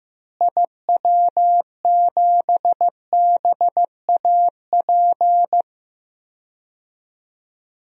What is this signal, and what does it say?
Morse code, heard as a keyed tone, about 15 wpm, 700 Hz. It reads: IW7BAP
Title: DXCC-100 Call Signs - 6 Characters - Repeated 3x - 15wpm